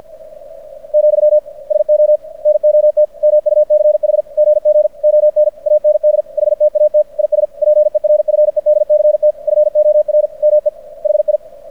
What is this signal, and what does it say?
Morse code, heard as a keyed tone, about 50 wpm, 600 Hz. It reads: , SO NOT MUCH GG ON AND HTAT IS PERFEDCT FOR ME HI
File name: archived / CRSnet